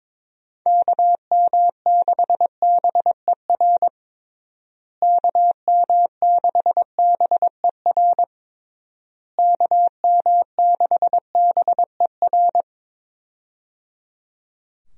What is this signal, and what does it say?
Morse code, heard as a keyed tone, about 22 wpm, 700 Hz. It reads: KM6BER KM6BER KM6BER
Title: USA Call Signs - 6 Characters - Repeated 3x - 22wpm